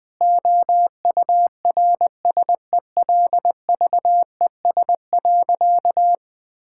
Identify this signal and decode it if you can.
Morse code, heard as a keyed tone, about 21 wpm, 700 Hz. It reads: OURSELVES.
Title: Sentences from Top 1600 Words 20wpm